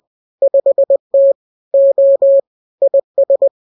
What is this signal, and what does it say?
Morse code, heard as a keyed tone, about 20 wpm, 550 Hz. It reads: 5T O IS